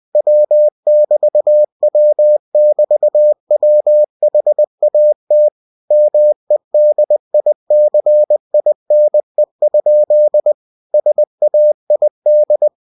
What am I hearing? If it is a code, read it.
Morse code, heard as a keyed tone, about 20 wpm, 600 Hz. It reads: W<BT>W<BT>WHAT MEDICINE? SAID
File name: cw-books / winnie-the-pooh / winnie-20wpm-page-0073